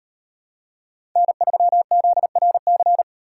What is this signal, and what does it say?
Morse code, heard as a keyed tone, about 38 wpm, 700 Hz. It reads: N3ZRC